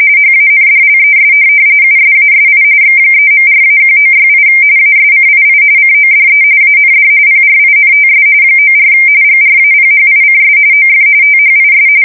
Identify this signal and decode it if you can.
RTTY, about 45.45 baud. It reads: HANDLES USASI X3.4-1968 ASCII CODE AND PRINTS 94 GRAPHICS OF THIS
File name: fsk170 / teletype-corp